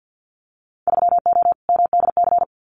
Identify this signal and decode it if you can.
Morse code, heard as a keyed tone, about 50 wpm, 700 Hz. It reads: 3O MDC
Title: Winter Field Day Exchanges Encore 50wpm